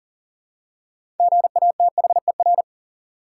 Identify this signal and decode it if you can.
Morse code, heard as a keyed tone, about 40 wpm, 700 Hz. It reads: GATHER
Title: Six Letter Words - 40wpm